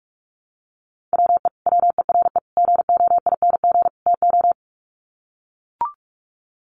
Morse code, heard as a keyed tone, about 44 wpm, 700 Hz. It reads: WE WERE GOING TO
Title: Sets of 4 Words 45wpm